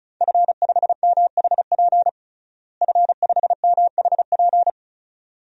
Morse code, heard as a keyed tone, about 34 wpm, 700 Hz. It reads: F5MHP F5MHP
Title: DXCC-100 Call Signs - 5 Characters - Repeated 2x - 35wpm